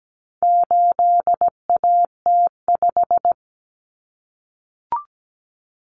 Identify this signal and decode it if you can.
Morse code, heard as a keyed tone, about 17 wpm, 700 Hz. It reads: 8AT5
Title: Common Cut Numbers - 4-Digit 17wpm